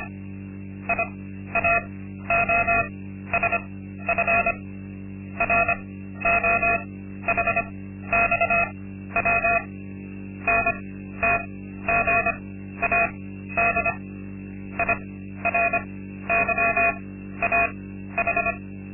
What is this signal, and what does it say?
Morse code, heard as a keyed tone, about 25 wpm, spaced as for 13 wpm, 690 Hz. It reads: IAOSF ROHXW NTGAD IRYAH
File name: Linksmakalnis-2